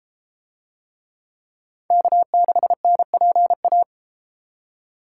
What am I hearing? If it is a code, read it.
Morse code, heard as a keyed tone, about 33 wpm, 700 Hz. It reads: K6NPA